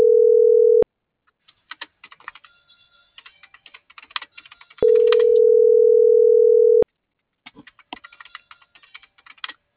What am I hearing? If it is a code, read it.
Ringback tone (North America), 440 + 480 Hz.
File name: CENTRAL MAYAB / out brandon.fi